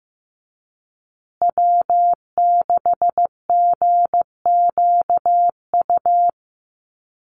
Morse code, heard as a keyed tone, about 15 wpm, 700 Hz. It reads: W6GQU
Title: USA Call Signs Repeated 2x 15wpm